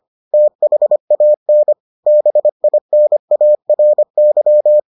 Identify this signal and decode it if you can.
Morse code, heard as a keyed tone, about 25 wpm, 600 Hz. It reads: THAN BINARY